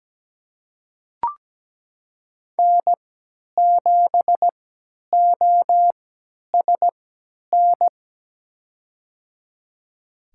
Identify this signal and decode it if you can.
Morse code, heard as a keyed tone, about 17 wpm, spaced as for 10 wpm, 700 Hz. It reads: N7OSN
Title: USA Call Signs - 5 Characters - 3x Character Spacing - 17wpm